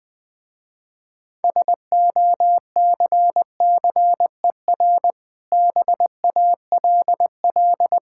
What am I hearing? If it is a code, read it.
Morse code, heard as a keyed tone, about 20 wpm, 700 Hz. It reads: SOCCER BALL
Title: Difficult Words with Hints - 20wpm